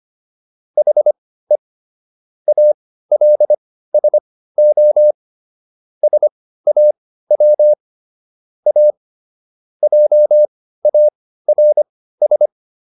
Morse code, heard as a keyed tone, about 25 wpm, spaced as for 15 wpm, 600 Hz. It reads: HE ALSO SAW A JARS